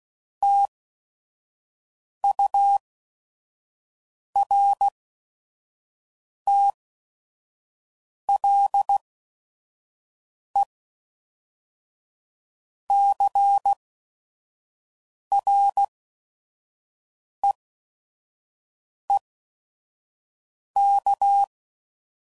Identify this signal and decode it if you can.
Morse code, heard as a keyed tone, about 16 wpm, spaced as for 5 wpm, 800 Hz. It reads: TURTLECREEK
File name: CW Code Groups @5WPM / fromSM's / Cities